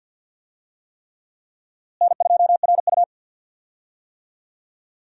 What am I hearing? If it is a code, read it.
Morse code, heard as a keyed tone, about 52 wpm, 700 Hz. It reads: N2RU
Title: USA Call Signs - 4 Characters - Repeated 3x - 50wpm